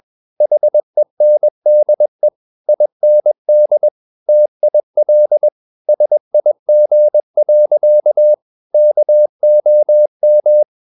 Morse code, heard as a keyed tone, about 21 wpm, 600 Hz. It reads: HENDE IND TIL SIG. KOM
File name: Ferie QRS 21wpm